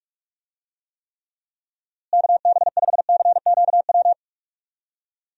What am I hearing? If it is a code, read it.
Morse code, heard as a keyed tone, about 42 wpm, 700 Hz. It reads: KB5CQW
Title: USA Call Signs - 6 Characters - Repeated 2x - 45wpm